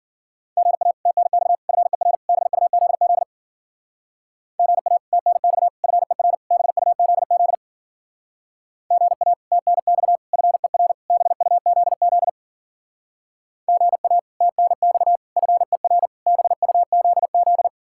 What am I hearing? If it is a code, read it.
Morse code, heard as a keyed tone, about 49 wpm, 700 Hz. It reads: GA TNX FER BUZZ GA TNX FER BUZZ GA TNX FER BUZZ GA TNX FER BUZZ
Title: Speed-Racing Common QSO Phrases - 40wpm